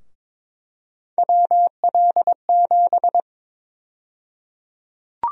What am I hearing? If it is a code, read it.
Morse code, heard as a keyed tone, about 21 wpm, 700 Hz. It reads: WL7
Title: USA Call Sign Prefixes - 22wpm